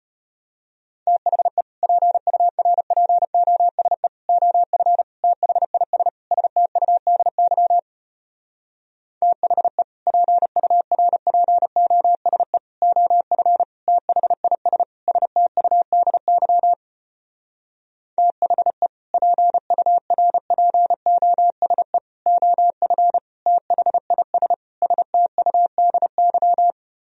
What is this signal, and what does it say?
Morse code, heard as a keyed tone, about 33 wpm, 700 Hz. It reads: THE PURPOSE OF THIS STUDY THE PURPOSE OF THIS STUDY THE PURPOSE OF THIS STUDY